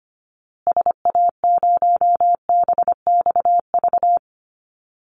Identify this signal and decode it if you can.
Morse code, heard as a keyed tone, about 25 wpm, 700 Hz. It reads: SA0BXV
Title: Call Signs Repeated 3x - 25wpm